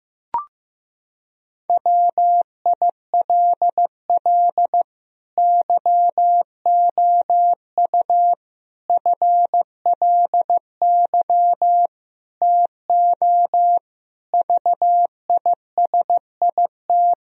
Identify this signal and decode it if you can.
Morse code, heard as a keyed tone, about 15 wpm, 700 Hz. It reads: WILL YOU FLY TO VISIT
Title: Sentences from Top 1000 Words - Review Part 2 - 15wpm